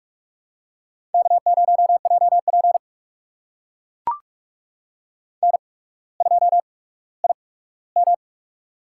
Morse code, heard as a keyed tone, about 44 wpm, spaced as for 19 wpm, 700 Hz. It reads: K0JP N2IM
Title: USA Call Signs - 8x Character Spacing - 45wpm